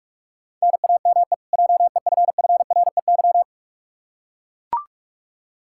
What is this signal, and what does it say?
Morse code, heard as a keyed tone, about 41 wpm, 700 Hz. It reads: NAME JEFFREY
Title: Practice QSO - Exchange 1 - Beginning - 45wpm